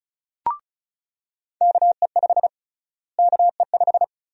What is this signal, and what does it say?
Morse code, heard as a keyed tone, about 36 wpm, 700 Hz. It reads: KE5 KE5
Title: USA Call Sign Prefixes - Repeated 2x - 35wpm